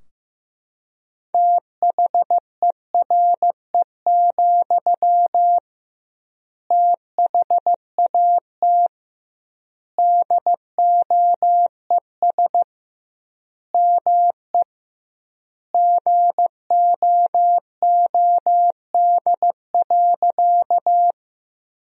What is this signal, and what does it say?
Morse code, heard as a keyed tone, about 15 wpm, 700 Hz. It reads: THERE, THAT DOES ME GOOD.
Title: 2x Word-Spacing - Sentences from Top 200 Words - 15wpm